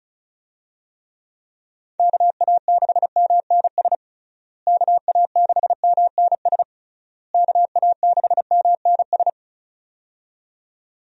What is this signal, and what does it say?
Morse code, heard as a keyed tone, about 36 wpm, 700 Hz. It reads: KA6MNS KA6MNS KA6MNS
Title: USA Call Signs - 6 Characters - Repeated 3x - 35wpm